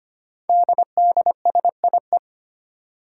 Morse code, heard as a keyed tone, about 25 wpm, 700 Hz. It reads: DDSIE
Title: Five Challenging Letter-Number Combinations 25wpm